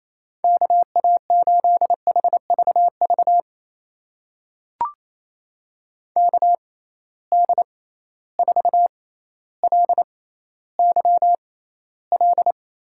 Morse code, heard as a keyed tone, about 28 wpm, spaced as for 14 wpm, 700 Hz. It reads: KA8HVV KD4LYL
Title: USA Call Signs - 6 Characters - 6x Character Spacing - 28wpm